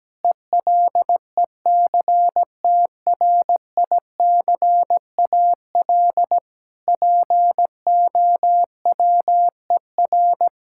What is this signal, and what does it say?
Morse code, heard as a keyed tone, about 17 wpm, 700 Hz. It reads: ELECTRICAL POWER
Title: Sentences from Top 1000 Words - Review Part 6 - 17wpm